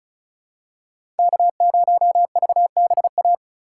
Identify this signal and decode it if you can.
Morse code, heard as a keyed tone, about 34 wpm, 700 Hz. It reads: K0VBA
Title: USA Call Signs - 5 Characters - 2x Character Spacing - 35wpm